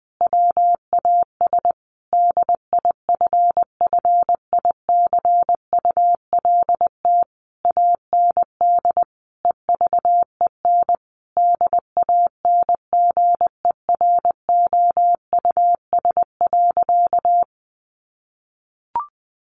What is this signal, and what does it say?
Morse code, heard as a keyed tone, about 20 wpm, 700 Hz. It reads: WAS DIFFICULT AND EVEN DANGEROUS.
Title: Sentences from Top 1200 Words 20wpm